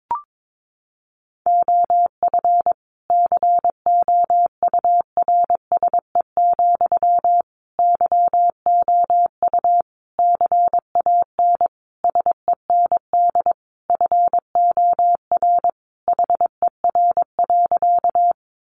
Morse code, heard as a keyed tone, about 22 wpm, 700 Hz. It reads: OF COURSE, YOU CAN SEND FOR HER.
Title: Sentences from Top 1500 Words 22wpm